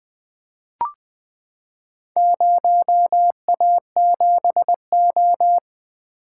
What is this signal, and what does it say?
Morse code, heard as a keyed tone, about 20 wpm, 700 Hz. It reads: 0A7O
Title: Four Letter-Number Combinations - 20wpm